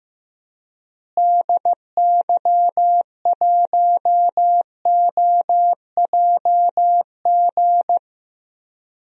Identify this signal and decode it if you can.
Morse code, heard as a keyed tone, about 15 wpm, 700 Hz. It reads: DY1OJG
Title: DXCC-100 Call Signs - 6 Characters - Repeated 2x - 15wpm